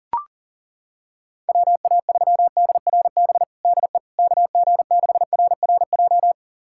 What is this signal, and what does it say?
Morse code, heard as a keyed tone, about 37 wpm, 700 Hz. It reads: WA3DRB DE KG6RRJ
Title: USA Callsign DE Callsign - 6 Characters - 40wpm